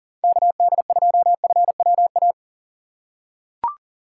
Morse code, heard as a keyed tone, about 39 wpm, 700 Hz. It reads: KD2FWA
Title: USA Call Signs - 6 Characters - 4x Character Spacing - 40wpm